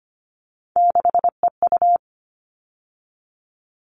Morse code, heard as a keyed tone, about 25 wpm, 700 Hz. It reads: 6EU
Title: All Cut Numbers - 3-Digit 25wpm